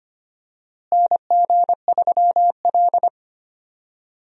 Morse code, heard as a keyed tone, about 25 wpm, 700 Hz. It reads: NG3L
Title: USA Call Signs - 4 Characters - Repeated 2x - 25wpm